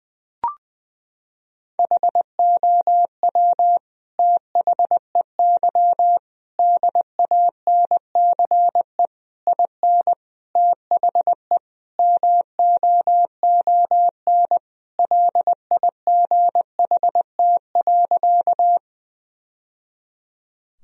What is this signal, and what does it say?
Morse code, heard as a keyed tone, about 19 wpm, 700 Hz. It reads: HOW THEY DANCE IN THE MOON LIGHT.
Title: Sentences from Top 1000 Words - Review Part 4 - 20wpm